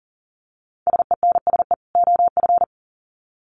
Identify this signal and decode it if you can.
Morse code, heard as a keyed tone, about 40 wpm, 700 Hz. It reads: SENSE OF
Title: Sets of 2 Words 40wpm